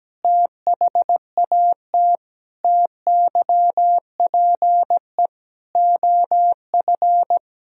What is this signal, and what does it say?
Morse code, heard as a keyed tone, about 17 wpm, 700 Hz. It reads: THAT TYPE OF
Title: Sentences from Top 1000 Words - Review Part 3 - 17wpm